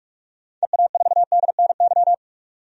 Morse code, heard as a keyed tone, about 46 wpm, 700 Hz. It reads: EA4DNY
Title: DXCC-100 Call Signs - 6 Characters - 45wpm